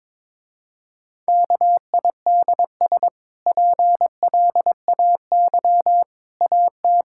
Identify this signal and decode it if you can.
Morse code, heard as a keyed tone, about 22 wpm, 700 Hz. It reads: KIDS PLAY AT
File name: practice / 22wpm / Easy Sentences with Hints - 22wpm